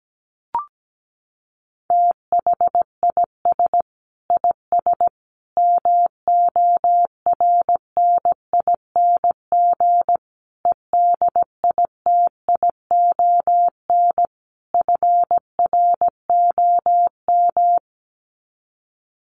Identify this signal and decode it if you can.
Morse code, heard as a keyed tone, about 17 wpm, 700 Hz. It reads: THIS IS MORNING EDITION FROM
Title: Sets of 5 Words Encore2 17wpm